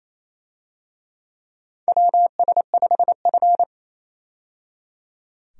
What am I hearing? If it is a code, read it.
Morse code, heard as a keyed tone, about 28 wpm, 690 Hz. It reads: WS5F
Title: USA Call Signs - 4 Characters - 28wpm